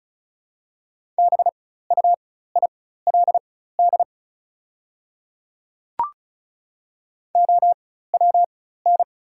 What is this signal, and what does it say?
Morse code, heard as a keyed tone, about 36 wpm, spaced as for 16 wpm, 690 Hz. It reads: BUILD OWN
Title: 4x Character Spacing - Top 500 Words - 35wpm